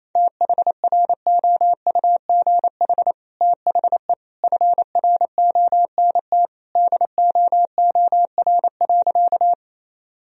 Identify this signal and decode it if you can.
Morse code, heard as a keyed tone, about 27 wpm, 700 Hz. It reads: THROUGH THE FRONT DOOR.
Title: Sentences from Top 1000 Words - Review Part 2 - 28wpm